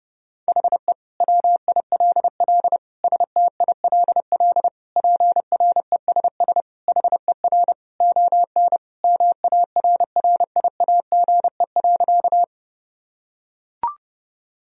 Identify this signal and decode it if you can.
Morse code, heard as a keyed tone, about 32 wpm, 700 Hz. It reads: HE WILL STILL PRESS HER ON MARRIAGE.
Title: Sentences from Top 1000 Words - Review Part 1 - 30wpm